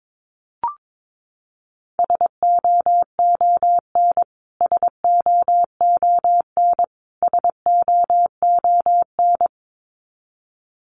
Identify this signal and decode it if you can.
Morse code, heard as a keyed tone, about 22 wpm, 700 Hz. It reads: SOON SOON SOON
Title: Top 500 Words Repeated 22wpm